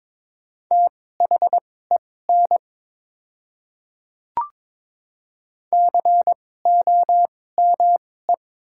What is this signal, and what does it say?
Morse code, heard as a keyed tone, about 21 wpm, spaced as for 16 wpm, 700 Hz. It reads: THEN COME
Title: 2x Character Spacing - Top 100 Words - 22wpm